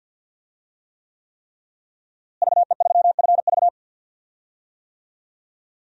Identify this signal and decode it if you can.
Morse code, heard as a keyed tone, about 47 wpm, 700 Hz. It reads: VE3FV
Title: DXCC-100 Call Signs - 5 Characters - Repeated 3x - 50wpm